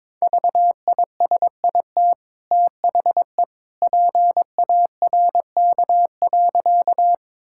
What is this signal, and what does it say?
Morse code, heard as a keyed tone, about 22 wpm, 690 Hz. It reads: VISIT THE PARK.
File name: practice / 22wpm / Easy Sentences with Hints - 22wpm